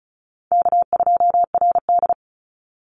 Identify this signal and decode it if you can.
Morse code, heard as a keyed tone, about 34 wpm, 700 Hz. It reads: K2RD
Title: Call Signs Repeated 3x - 35wpm